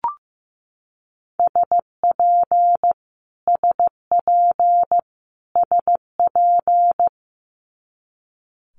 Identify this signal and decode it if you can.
Morse code, heard as a keyed tone, about 15 wpm, 700 Hz. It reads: SP SP SP
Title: Multi-letter Phonemes Repeated 3x 15wpm